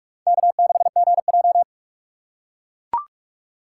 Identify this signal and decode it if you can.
Morse code, heard as a keyed tone, about 44 wpm, 690 Hz. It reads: K6GJ
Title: USA Call Signs - 4x Character Spacing - 45wpm